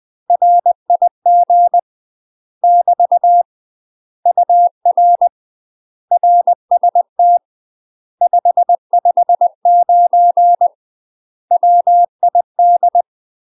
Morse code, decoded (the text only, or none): RIG <BT> UR RST 559 WID